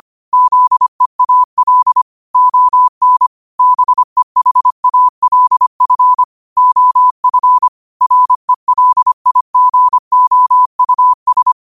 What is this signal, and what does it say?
Morse code, heard as a keyed tone, about 25 wpm, 1000 Hz. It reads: ZEAL ON BEHALF OF RELIGOUS
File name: qotdfiles / 25wpm